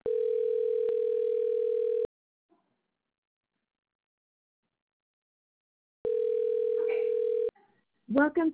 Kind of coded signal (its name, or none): telephone call progress tone